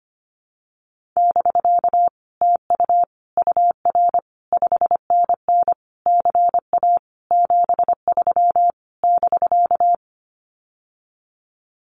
Morse code, heard as a keyed tone, about 25 wpm, 700 Hz. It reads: <BK> TU UR 5NN CA 73 <BK>
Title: Common QSO Phrases Encore 25wpm